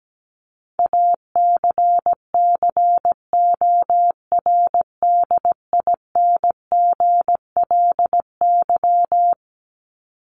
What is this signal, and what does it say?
Morse code, heard as a keyed tone, about 17 wpm, 700 Hz. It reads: ACCORDINGLY
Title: LY Suffix - Top 100 Words with Base Words 17wpm